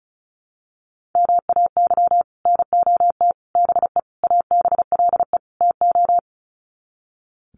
Morse code, heard as a keyed tone, about 34 wpm, 700 Hz. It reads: MAY NOT BE ABLE TO
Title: Sets of 5 Words 35wpm